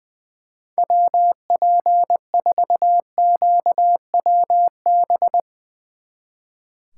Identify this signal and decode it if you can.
Morse code, heard as a keyed tone, about 23 wpm, 700 Hz. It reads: WP4QWB
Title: Speed Racing - USA 2x3 Call Signs 20wpm